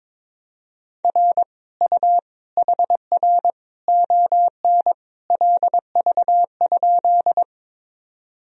Morse code, heard as a keyed tone, about 22 wpm, 700 Hz. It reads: R U HR ON LV?
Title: Sentences from Top 400 Words and Abbreviations 22wpm